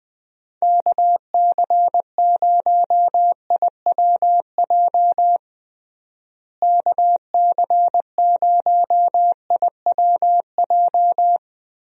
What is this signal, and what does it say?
Morse code, heard as a keyed tone, about 21 wpm, 700 Hz. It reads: KC0IWJ KC0IWJ
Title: USA Call Signs - 6 Characters - Repeated 3x - 20wpm